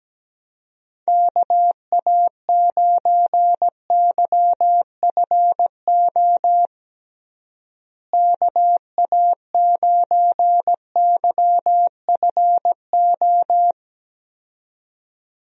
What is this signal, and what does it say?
Morse code, heard as a keyed tone, about 17 wpm, 700 Hz. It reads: KA9YFO KA9YFO
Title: USA Call Signs - 6 Characters - Repeated 2x - 17wpm